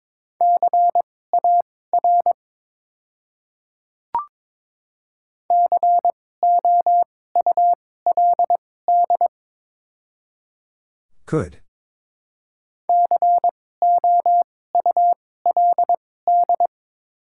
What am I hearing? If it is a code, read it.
Morse code, heard as a keyed tone, about 22 wpm, spaced as for 16 wpm, 700 Hz. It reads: CAR COULD COULD
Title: 2x Character Spacing - Top 300 Words - 22wpm